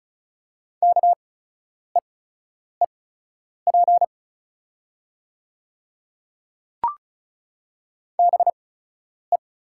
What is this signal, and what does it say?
Morse code, heard as a keyed tone, about 34 wpm, spaced as for 10 wpm, 700 Hz. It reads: KEEP BE
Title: 8x Character Spacing - Top 300 Words - 35wpm